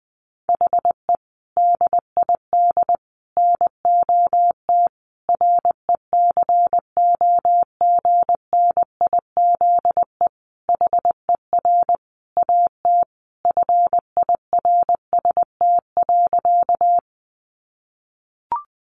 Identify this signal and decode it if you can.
Morse code, heard as a keyed tone, about 20 wpm, 700 Hz. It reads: HE DID NOT RECOGNIZE HER AT FIRST.
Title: Sentences from Top 1400 Words 20wpm